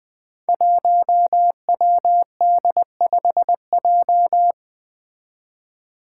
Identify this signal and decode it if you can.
Morse code, heard as a keyed tone, about 20 wpm, 700 Hz. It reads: 1WD5J
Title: Five Challenging Letter-Number Combinations 20wpm